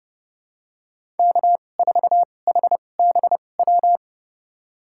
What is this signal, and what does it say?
Morse code, heard as a keyed tone, about 30 wpm, spaced as for 22 wpm, 700 Hz. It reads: K4HBW
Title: USA Call Signs - 5 Characters - 2x Character Spacing - 30wpm